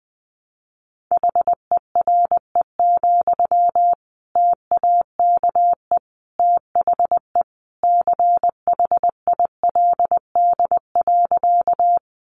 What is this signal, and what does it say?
Morse code, heard as a keyed tone, about 20 wpm, 700 Hz. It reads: HERE, TAKE THE CHILD.
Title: Sentences from Top 800 Words 20wpm